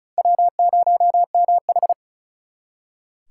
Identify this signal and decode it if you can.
Morse code, heard as a keyed tone, about 47 wpm, spaced as for 41 wpm, 700 Hz. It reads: W0MH
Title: Speed Racing - USA 1x2 Call Signs 35wpm